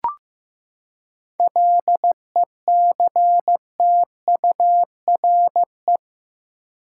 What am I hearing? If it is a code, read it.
Morse code, heard as a keyed tone, about 15 wpm, 700 Hz. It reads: LECTURE
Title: Seven Letter Words - 15wpm